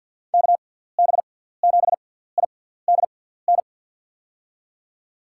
Morse code, heard as a keyed tone, about 52 wpm, spaced as for 17 wpm, 690 Hz. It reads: KB7IDN
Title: USA Call Signs - 6x Character Spacing - 50wpm